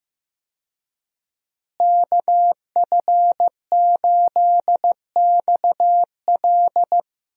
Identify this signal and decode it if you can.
Morse code, heard as a keyed tone, about 15 wpm, 700 Hz. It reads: KF8XL
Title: USA Call Signs - 5 Characters - Repeated 2x - 15wpm